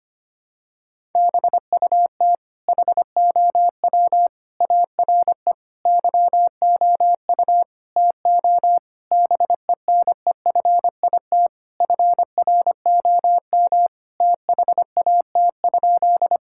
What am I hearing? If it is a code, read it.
Morse code, heard as a keyed tone, about 25 wpm, 700 Hz. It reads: BUT HOW ARE YOU TO BENEFIT FROM THAT?
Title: Sentences from Top 1000 Words - Review Part 6 - 25wpm